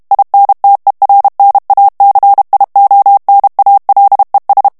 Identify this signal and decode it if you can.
Morse code, heard as a keyed tone, about 32 wpm, 800 Hz. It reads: INTERNACIONALES
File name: ESPAGNOL TEXTE EN CLAIR 2 VITESSE 30 MOTS MINUTES